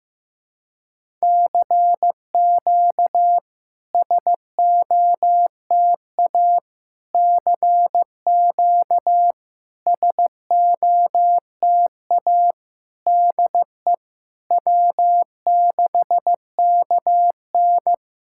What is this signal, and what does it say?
Morse code, heard as a keyed tone, about 16 wpm, 700 Hz. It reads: CQ SOTA CQ SOTA DE W6KN
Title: SOTA QSO Practice 15wpm